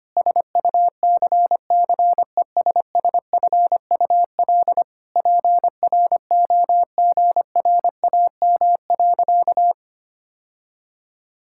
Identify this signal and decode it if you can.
Morse code, heard as a keyed tone, about 25 wpm, 700 Hz. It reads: SUCCESSFUL PROGRAM.
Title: Sentences from Top 1000 Words - Review Part 5 - 25wpm